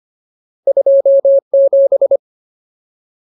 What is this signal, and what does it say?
Morse code, heard as a keyed tone, about 25 wpm, 550 Hz. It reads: 27
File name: Ultimate Word List / 25WPM Ultimate